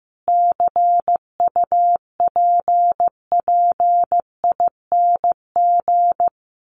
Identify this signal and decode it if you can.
Morse code, heard as a keyed tone, about 15 wpm, 700 Hz. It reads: CUPPING
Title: A Sound Of Thunder 15wpm